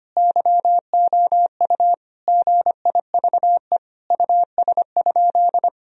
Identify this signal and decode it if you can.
Morse code, heard as a keyed tone, about 25 wpm, 700 Hz. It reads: YOU GIVE US?
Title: Sentences from Top 1000 Words - Review Part 4 - 25wpm